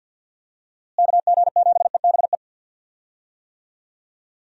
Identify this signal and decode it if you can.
Morse code, heard as a keyed tone, about 50 wpm, 700 Hz. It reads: KG7EBE
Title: USA Call Signs - 6 Characters - Repeated 3x - 50wpm